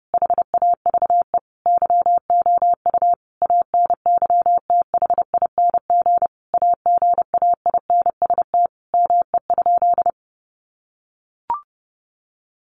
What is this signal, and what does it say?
Morse code, heard as a keyed tone, about 30 wpm, 700 Hz. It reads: HAVE YOU ANYTHING AGAINST ME?
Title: Sentences from Top 1000 Words 30wpm